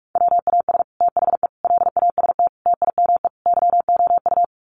Morse code, heard as a keyed tone, about 44 wpm, 690 Hz. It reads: WAS THE LAST TIME YOU
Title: Sets of 5 Words Encore2 45wpm